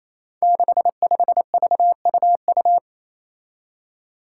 Morse code, heard as a keyed tone, about 28 wpm, 700 Hz. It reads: 65VUU
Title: Five Challenging Letter-Number Combinations 28wpm